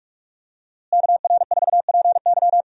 Morse code, heard as a keyed tone, about 46 wpm, 700 Hz. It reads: KR4PY
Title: USA Call Signs - 5 Characters - Repeated 2x - 45wpm